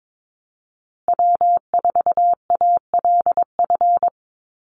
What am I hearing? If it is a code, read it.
Morse code, heard as a keyed tone, about 22 wpm, 700 Hz. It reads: W4ALF
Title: Call Signs Repeated 2x - 22wpm